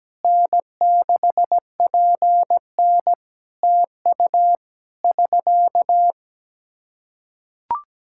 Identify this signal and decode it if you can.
Morse code, heard as a keyed tone, about 17 wpm, 700 Hz. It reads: N6PN TU <SK>
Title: K1USN SST QSO Practice 17wpm